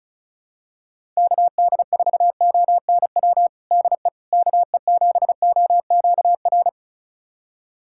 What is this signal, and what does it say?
Morse code, heard as a keyed tone, about 34 wpm, 700 Hz. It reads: KD4ONW DE KE7OQR
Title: USA Callsign DE Callsign - 6 Characters - 35wpm